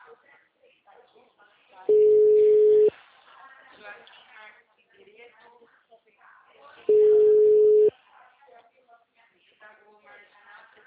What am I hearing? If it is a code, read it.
Ringback tone (Europe, 425 Hz), with speech.